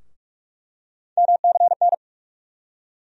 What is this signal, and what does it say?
Morse code, heard as a keyed tone, about 44 wpm, 700 Hz. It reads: MCN
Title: USA Call Sign Postfixes - 3 Letters 45wpm